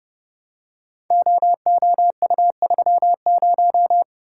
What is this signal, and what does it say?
Morse code, heard as a keyed tone, about 30 wpm, 700 Hz. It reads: OOU30
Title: Code Groups - 5 Letter:Numbers - 30wpm